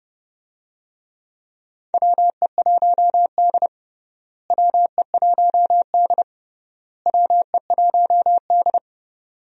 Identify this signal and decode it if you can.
Morse code, heard as a keyed tone, about 30 wpm, 700 Hz. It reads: WE1D WE1D WE1D
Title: USA Call Signs - 4 Characters - Repeated 3x - 30wpm